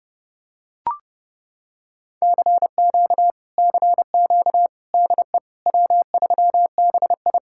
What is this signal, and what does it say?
Morse code, heard as a keyed tone, about 29 wpm, 700 Hz. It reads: CQ CQ DE W3BI